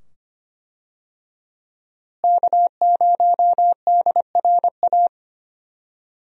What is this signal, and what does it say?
Morse code, heard as a keyed tone, about 25 wpm, 700 Hz. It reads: K0DRA
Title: USA Call Signs - 5 Characters - Repeated 2x - 25wpm